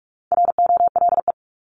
Morse code, heard as a keyed tone, about 44 wpm, 700 Hz. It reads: ROLE
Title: Top 751-1000 Words 45wpm